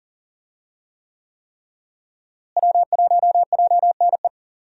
Morse code, heard as a keyed tone, about 40 wpm, 700 Hz. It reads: W1JNE
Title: USA Call Signs - 5 Characters - Repeated 3x - 40wpm